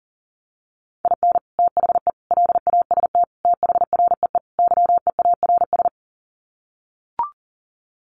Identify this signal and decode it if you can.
Morse code, heard as a keyed tone, about 38 wpm, 700 Hz. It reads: IN THE LAST THREE YEARS
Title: Sets of 5 Words Encore2 40wpm